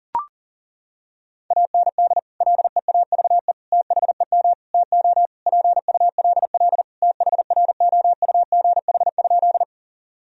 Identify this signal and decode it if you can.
Morse code, heard as a keyed tone, about 40 wpm, 700 Hz. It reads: AND LEAVE THEM TO PULL THROUGH?
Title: Sentences from Top 1000 Words - Review Part 2 - 40wpm